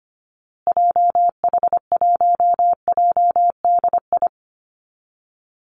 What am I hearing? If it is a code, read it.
Morse code, heard as a keyed tone, about 25 wpm, 700 Hz. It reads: JH1JDI
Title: Call Signs Repeated 2x - 25wpm